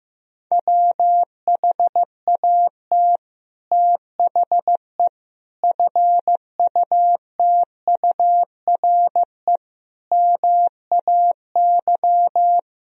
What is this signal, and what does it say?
Morse code, heard as a keyed tone, about 14 wpm, 700 Hz. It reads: WHAT THE FUTURE MAY
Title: Sentences from Top 1000 Words - Review Part 2 - 15wpm